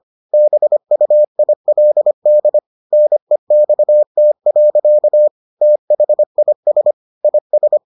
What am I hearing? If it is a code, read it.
Morse code, heard as a keyed tone, about 25 wpm, 600 Hz. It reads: BUILD NEXT. THIS IS